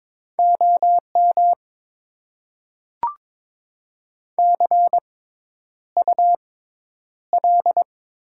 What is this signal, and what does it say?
Morse code, heard as a keyed tone, about 22 wpm, spaced as for 10 wpm, 700 Hz. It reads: OM CUL
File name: Top 90 QSO Elements - 6x Character Spacing - 22wpm